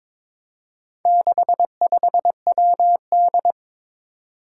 Morse code, heard as a keyed tone, about 22 wpm, 700 Hz. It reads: 65WD
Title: Four Challenging Letter-Number Combinations 22wpm